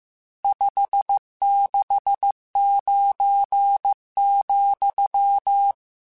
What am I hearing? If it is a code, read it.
Morse code, heard as a keyed tone, about 15 wpm, 800 Hz. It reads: 569,
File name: RSTs / RST-mp3s / Various RST @13